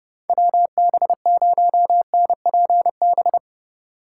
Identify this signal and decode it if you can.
Morse code, heard as a keyed tone, about 30 wpm, 700 Hz. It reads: WB0NPB
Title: USA Call Signs - 6 Characters - 4x Character Spacing - 30wpm